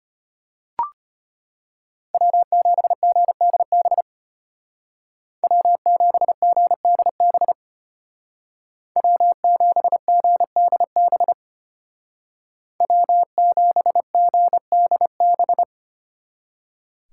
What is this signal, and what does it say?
Morse code, heard as a keyed tone, about 30 wpm, 700 Hz. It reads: W7GDB W7GDB W7GDB W7GDB
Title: Speed Racing - USA Call Signs 25wpm